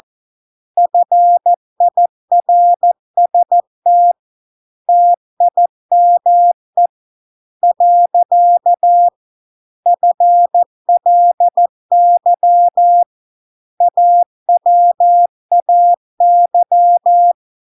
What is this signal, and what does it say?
Morse code, heard as a keyed tone, about 14 wpm, 700 Hz. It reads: FIRST TIME . FLY AWAY